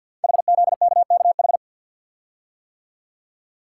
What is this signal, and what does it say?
Morse code, heard as a keyed tone, about 50 wpm, 690 Hz. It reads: HZKKH
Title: Code Groups - 5 Letters - 50wpm